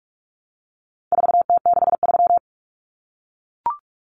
Morse code, heard as a keyed tone, about 44 wpm, 700 Hz. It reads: 4T63
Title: Common Cut Numbers - 4-Digit 45wpm